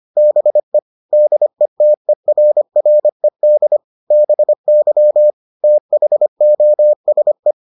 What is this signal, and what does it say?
Morse code, heard as a keyed tone, about 25 wpm, 600 Hz. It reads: BE DETERRED BY THOSE